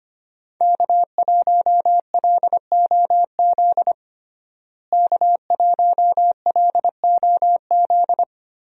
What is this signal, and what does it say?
Morse code, heard as a keyed tone, about 25 wpm, 700 Hz. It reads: K1LOZ K1LOZ
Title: USA Call Signs - 5 Characters - Repeated 3x - 25wpm